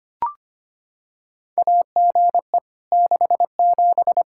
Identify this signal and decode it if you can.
Morse code, heard as a keyed tone, about 25 wpm, 700 Hz. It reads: AGE 67
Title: Practice QSO - Exchange 3 - More Info - 25wpm